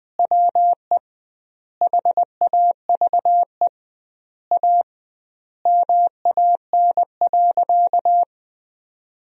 Morse code, heard as a keyed tone, about 21 wpm, 700 Hz. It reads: WE HAVE A MAN.
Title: 2x Word-Spacing - Sentences from Top 300 Words - 20wpm